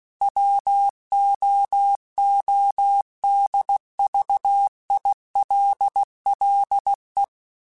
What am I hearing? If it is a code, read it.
Morse code, heard as a keyed tone, about 16 wpm, 800 Hz. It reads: WOODVILLE
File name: fromSMs / FromSMs mp3s / Cities@15